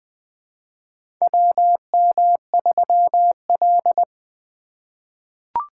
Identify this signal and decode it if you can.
Morse code, heard as a keyed tone, about 20 wpm, 700 Hz. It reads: WM3L
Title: USA Call Signs - 4 Characters - 3x Character Spacing - 20wpm